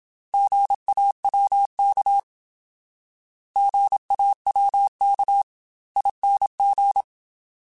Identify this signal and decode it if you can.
Morse code, heard as a keyed tone, about 27 wpm, 800 Hz. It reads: GAWK GAWKING
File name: Alphabetical / Alphabetical mp3s / Gs @25